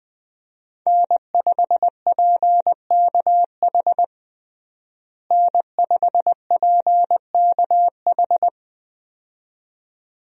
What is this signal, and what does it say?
Morse code, heard as a keyed tone, about 21 wpm, 700 Hz. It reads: N5PKH N5PKH
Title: USA Call Signs - 5 Characters - Repeated 2x - 20wpm